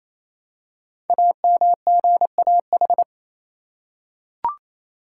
Morse code, heard as a keyed tone, about 28 wpm, 700 Hz. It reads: AMGAH
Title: Code Groups - 5 Letters - 28wpm